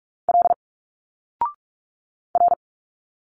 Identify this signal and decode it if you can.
Morse code, heard as a keyed tone, about 46 wpm, 700 Hz. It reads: L R
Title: Single Letters 45wpm